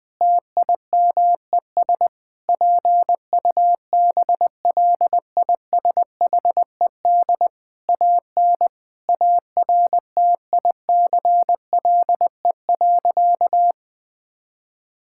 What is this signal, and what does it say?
Morse code, heard as a keyed tone, about 21 wpm, 700 Hz. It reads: TIMES PUBLISHED AN ARTICLE.
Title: Sentences with 4-Word Phrases - 20wpm